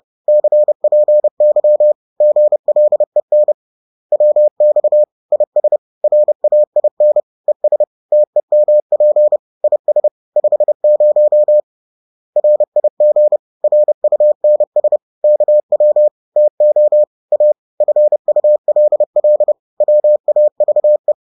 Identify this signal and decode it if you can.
Morse code, heard as a keyed tone, about 30 wpm, 600 Hz. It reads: CPY GLEN WX IS RAIN ES TEMP IS 50 RIG RUNS KW TO A FULL WAVE